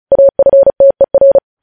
Morse code, heard as a keyed tone, about 36 wpm, 550 Hz. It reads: AFTER
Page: News headlines in Morse code at 35 wpm